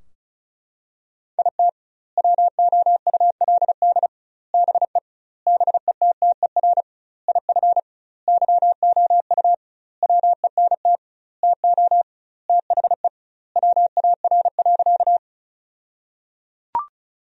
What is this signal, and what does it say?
Morse code, heard as a keyed tone, about 34 wpm, 700 Hz. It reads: IT WOULD BE BETTER IF YOU WENT TO THE WAR.
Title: 2x Word-Spacing - Sentences from Top 400 Words - 35wpm